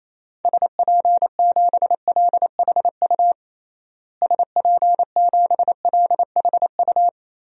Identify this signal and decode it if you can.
Morse code, heard as a keyed tone, about 28 wpm, 700 Hz. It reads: SP7LHU SP7LHU
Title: DXCC-100 Call Signs - 6 Characters - Repeated 3x - 28wpm